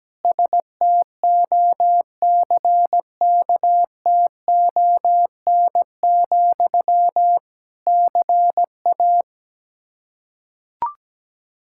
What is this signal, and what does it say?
Morse code, heard as a keyed tone, about 17 wpm, 700 Hz. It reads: STOCKTON, CA
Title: Practice QSO - Exchange 1 - Beginning - 17wpm